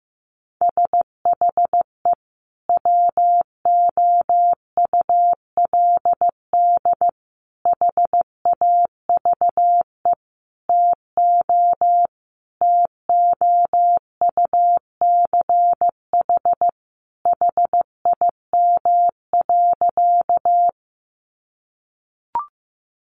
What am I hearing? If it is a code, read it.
Morse code, heard as a keyed tone, about 15 wpm, 700 Hz. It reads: SHE WOULD HAVE TO TOUCH HIM.
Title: Sentences from Top 1200 Words 15wpm